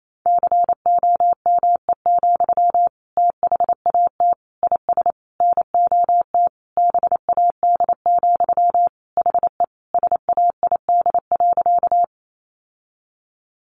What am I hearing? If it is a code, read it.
Morse code, heard as a keyed tone, about 28 wpm, 700 Hz. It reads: COME, THAT IS NOT BAD, HE SAID.
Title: Sentences from Top 700 Words 28wpm